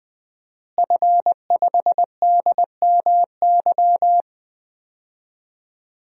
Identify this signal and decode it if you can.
Morse code, heard as a keyed tone, about 20 wpm, 700 Hz. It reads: F5DMY
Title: DXCC-100 Call Signs - 5 Characters - Repeated 2x - 20wpm